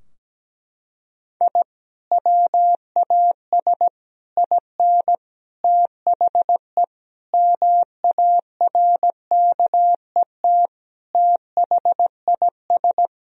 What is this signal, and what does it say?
Morse code, heard as a keyed tone, about 17 wpm, 700 Hz. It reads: I WAS IN THE MARKET THIS
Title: Sentences from Top 1000 Words - Review Part 3 - 17wpm